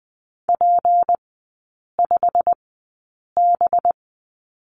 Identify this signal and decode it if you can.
Morse code, heard as a keyed tone, about 20 wpm, spaced as for 8 wpm, 700 Hz. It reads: P5B
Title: 184 Lesson 30e - Letter b - Three Characters 20wpm